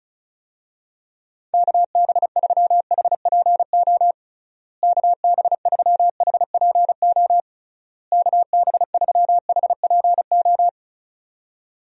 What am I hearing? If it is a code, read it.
Morse code, heard as a keyed tone, about 36 wpm, 700 Hz. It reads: KB3HPO KB3HPO KB3HPO
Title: USA Call Signs - 6 Characters - Repeated 3x - 35wpm